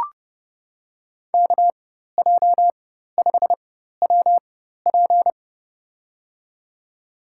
Morse code, heard as a keyed tone, about 30 wpm, spaced as for 14 wpm, 700 Hz. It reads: KJ5WP
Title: USA Call Signs - 5 Characters - 4x Character Spacing - 30wpm